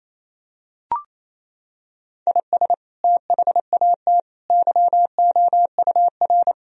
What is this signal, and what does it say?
Morse code, heard as a keyed tone, about 28 wpm, 700 Hz. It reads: IS THAT YOUR
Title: Sentences from Top 1000 Words - Review Part 3 - 28wpm